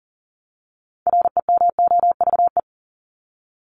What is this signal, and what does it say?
Morse code, heard as a keyed tone, about 40 wpm, 700 Hz. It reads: REMOVE
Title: Top 751-1000 Words 40wpm